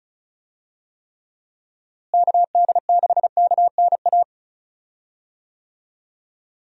Morse code, heard as a keyed tone, about 36 wpm, 700 Hz. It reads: KD6KNA
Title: USA Call Signs - 6 Characters - Repeated 2x - 35wpm